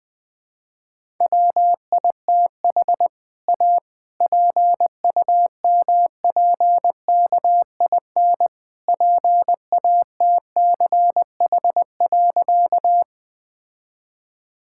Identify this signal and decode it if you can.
Morse code, heard as a keyed tone, about 20 wpm, 700 Hz. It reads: WITH A PUMPKIN PATCH.
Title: All-Age Appropriate Jokes - 20wpm